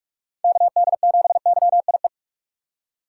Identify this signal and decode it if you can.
Morse code, heard as a keyed tone, about 42 wpm, 700 Hz. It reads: KD7YIE
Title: USA Call Signs - 6 Characters - 8x Character Spacing - 45wpm